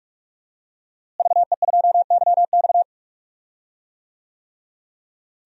Morse code, heard as a keyed tone, about 44 wpm, 700 Hz. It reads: VE2YX
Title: DXCC-100 Call Signs - 5 Characters - Repeated 2x - 45wpm